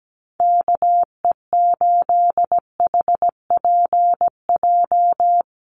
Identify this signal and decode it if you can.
Morse code, heard as a keyed tone, about 17 wpm, 700 Hz. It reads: KE8HPJ
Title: USA Call Signs Repeated 3x 17wpm